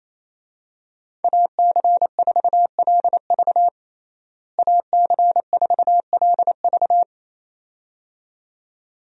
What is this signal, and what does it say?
Morse code, heard as a keyed tone, about 28 wpm, 700 Hz. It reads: AC4LV AC4LV
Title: USA Call Signs - 5 Characters - Repeated 2x - 28wpm